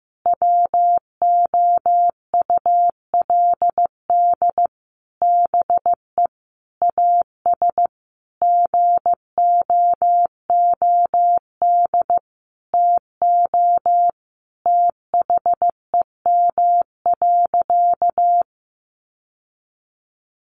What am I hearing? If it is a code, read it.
Morse code, heard as a keyed tone, about 15 wpm, 700 Hz. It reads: WOULD BE AS GOOD TO THEM.
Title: Sentences from Top 1200 Words 15wpm